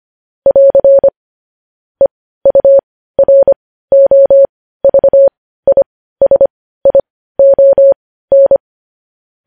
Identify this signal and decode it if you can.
Morse code, heard as a keyed tone, about 25 wpm, spaced as for 15 wpm, 550 Hz. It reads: <AR> EUROVISION
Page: news-headlines-in-morse-code-at-15-wpm